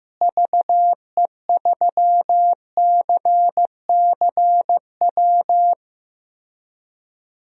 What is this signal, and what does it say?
Morse code, heard as a keyed tone, about 15 wpm, 700 Hz. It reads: VE3CCW
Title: DXCC-100 Call Signs - 6 Characters - Repeated 3x - 15wpm